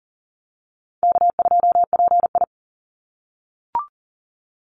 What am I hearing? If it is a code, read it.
Morse code, heard as a keyed tone, about 37 wpm, 700 Hz. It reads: K2PI
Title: Easy Call Signs - 40wpm